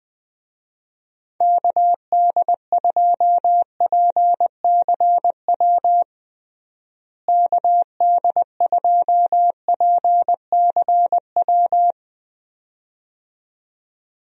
Morse code, heard as a keyed tone, about 20 wpm, 700 Hz. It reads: KD2PCW KD2PCW
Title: USA Call Signs - 6 Characters - Repeated 2x - 20wpm